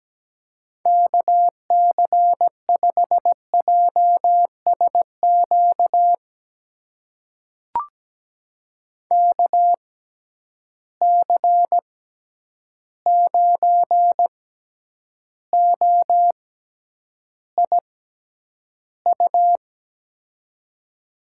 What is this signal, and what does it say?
Morse code, heard as a keyed tone, about 17 wpm, spaced as for 9 wpm, 700 Hz. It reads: KC5JSQ KC9OIU
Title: USA Call Signs - 6x Character Spacing - 17wpm